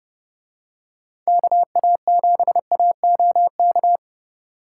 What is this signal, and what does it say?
Morse code, heard as a keyed tone, about 30 wpm, 700 Hz. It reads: KA7AOK